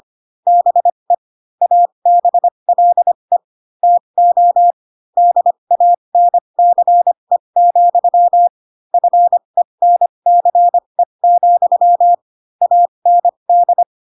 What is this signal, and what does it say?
Morse code, heard as a keyed tone, about 25 wpm, spaced as for 22 wpm, 700 Hz. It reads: BE ABLE TO DANCE, FENCE, AND
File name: Pinocchio / Pinocchio ch2 18 25 ik1hge